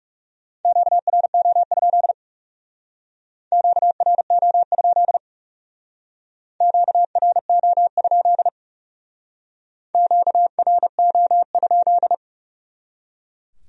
Morse code, heard as a keyed tone, about 38 wpm, 700 Hz. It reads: QRO? QRO? QRO? QRO?